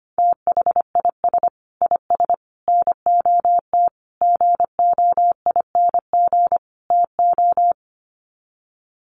Morse code, heard as a keyed tone, about 25 wpm, 700 Hz. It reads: THIS IS NOT GOING TO
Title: Sets of 5 Words Encore2 25wpm